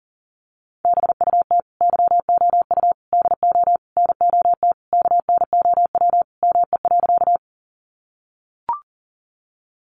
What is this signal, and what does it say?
Morse code, heard as a keyed tone, about 37 wpm, 700 Hz. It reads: BUT YOU DO NOT KNOW ME.
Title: Sentences from Top 600 Words 40wpm